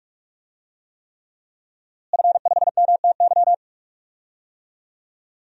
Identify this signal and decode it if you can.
Morse code, heard as a keyed tone, about 44 wpm, 700 Hz. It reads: F5MTY
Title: DXCC-100 Call Signs - 5 Characters - Repeated 3x - 45wpm